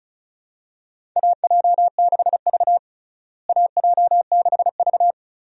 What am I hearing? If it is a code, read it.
Morse code, heard as a keyed tone, about 36 wpm, 700 Hz. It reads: AJ6V AJ6V
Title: USA Call Signs - 4 Characters - Repeated 3x - 35wpm